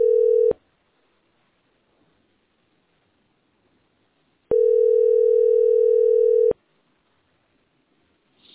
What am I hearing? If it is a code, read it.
Ringback tone (North America), 440 + 480 Hz.